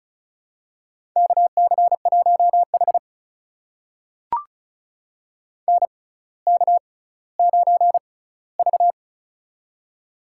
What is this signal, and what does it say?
Morse code, heard as a keyed tone, about 36 wpm, spaced as for 18 wpm, 690 Hz. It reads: KC1H NK9V
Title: USA Call Signs - 4 Characters - 6x Character Spacing - 35wpm